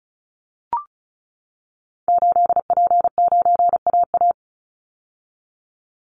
Morse code, heard as a keyed tone, about 34 wpm, 700 Hz. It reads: 8P9AA
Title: Call Signs - 35wpm